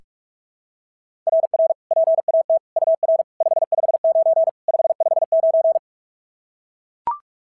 Morse code, heard as a keyed tone, about 44 wpm, 690 Hz. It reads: RR PAT UR 559 559
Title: SOTA QSO Practice 45wpm